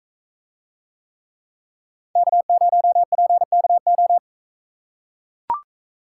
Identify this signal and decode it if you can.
Morse code, heard as a keyed tone, about 40 wpm, 700 Hz. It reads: K0PKO